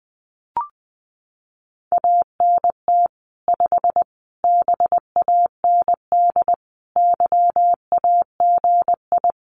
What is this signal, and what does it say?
Morse code, heard as a keyed tone, about 20 wpm, 700 Hz. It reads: ANT 5 BAND YAGI
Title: Common QSO Phrases 20wpm